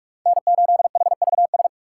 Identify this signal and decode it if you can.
Morse code, heard as a keyed tone, about 41 wpm, 700 Hz. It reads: N8HVS
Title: USA Call Signs - 5 Characters - 3x Character Spacing - 45wpm